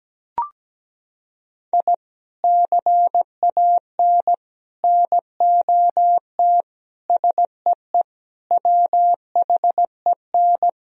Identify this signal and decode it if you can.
Morse code, heard as a keyed tone, about 17 wpm, 700 Hz. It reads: I CAN NOT SEE WHEN
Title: Sentences from Top 1000 Words - Review Part 4 - 17wpm